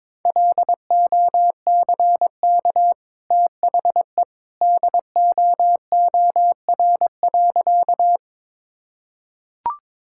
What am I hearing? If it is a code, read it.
Morse code, heard as a keyed tone, about 22 wpm, 700 Hz. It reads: LOCK THE DOOR.
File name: practice / 22wpm / Easy Sentences with Hints - 22wpm